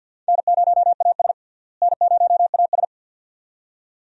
Morse code, heard as a keyed tone, about 47 wpm, 700 Hz. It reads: N0AS N0AS
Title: USA Call Signs - 4 Characters - Repeated 2x - 50wpm